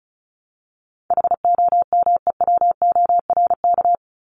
Morse code, heard as a keyed tone, about 34 wpm, 700 Hz. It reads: HOMEWORK
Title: Top 1501-1750 Words 35wpm